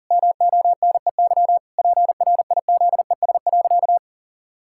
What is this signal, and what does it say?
Morse code, heard as a keyed tone, about 40 wpm, 700 Hz. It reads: MONEY PRIZES.
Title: Pangram Challenge v2 - 40wpm